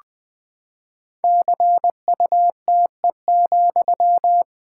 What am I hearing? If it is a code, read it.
Morse code, heard as a keyed tone, about 20 wpm, 700 Hz. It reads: CUTE,
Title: Pangram Challenge v2 - 20wpm